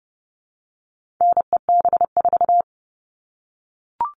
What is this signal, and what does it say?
Morse code, heard as a keyed tone, about 30 wpm, 700 Hz. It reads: NEB4
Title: All Cut Numbers - 4-Digit 30wpm